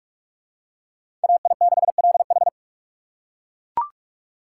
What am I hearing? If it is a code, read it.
Morse code, heard as a keyed tone, about 41 wpm, 700 Hz. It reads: AI6LH